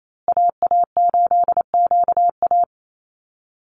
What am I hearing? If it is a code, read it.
Morse code, heard as a keyed tone, about 28 wpm, 700 Hz. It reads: AA8QA
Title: USA Call Signs Repeated 3x 28wpm